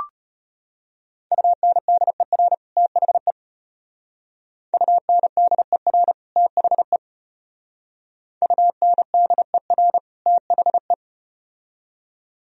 Morse code, heard as a keyed tone, about 33 wpm, 700 Hz. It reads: UNDER THE UNDER THE UNDER THE